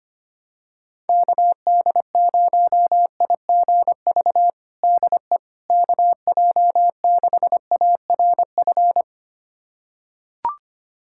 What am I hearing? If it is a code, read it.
Morse code, heard as a keyed tone, about 25 wpm, 700 Hz. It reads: KD0IGV DE KJ6ARF
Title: USA Callsign DE Callsign - 6 Characters - 25wpm